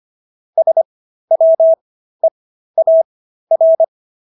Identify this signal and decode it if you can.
Morse code, heard as a keyed tone, about 25 wpm, spaced as for 13 wpm, 650 Hz. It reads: SWEAR
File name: SS106 13